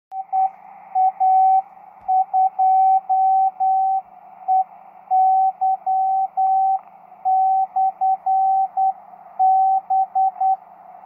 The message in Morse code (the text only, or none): EA2EY/B